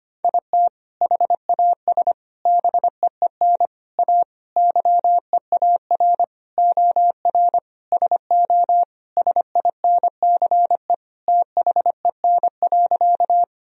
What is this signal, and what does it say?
Morse code, heard as a keyed tone, about 25 wpm, 700 Hz. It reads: IT HAS BEEN A YEAR OR SO SINCE THEN.
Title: Sentences from Top 1000 Words - Review Part 1 - 25wpm